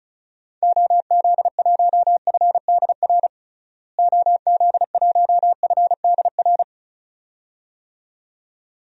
Morse code, heard as a keyed tone, about 34 wpm, 700 Hz. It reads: OZ1FDR OZ1FDR
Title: DXCC-100 Call Signs - 6 Characters - Repeated 2x - 35wpm